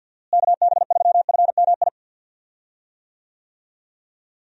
Morse code, heard as a keyed tone, about 50 wpm, 700 Hz. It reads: KB3FMI
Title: USA Call Signs - 6 Characters - Repeated 2x - 50wpm